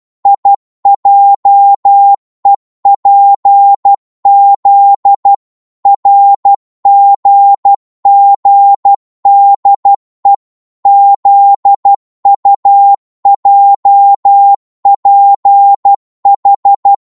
Morse code, heard as a keyed tone, about 12 wpm, 800 Hz. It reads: IJEPZ RGGDE ZUJPH
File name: Morse / ltr-12wpm